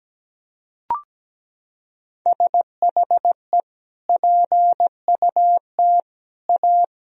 Morse code, heard as a keyed tone, about 17 wpm, 700 Hz. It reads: SHE PUT A
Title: Sentences from Top 1000 Words - Review Part 2 - 17wpm